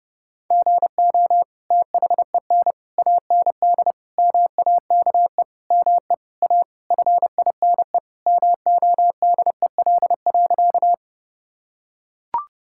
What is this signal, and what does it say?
Morse code, heard as a keyed tone, about 29 wpm, 700 Hz. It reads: GO THEN AND MAKE ME A FINE MODEL.
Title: Sentences from Top 1000 Words - Review Part 5 - 30wpm